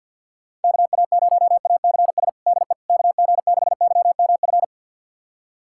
Morse code, heard as a keyed tone, about 50 wpm, 700 Hz. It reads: KA0ACS DE KG6YMF
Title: USA Callsign DE Callsign 50wpm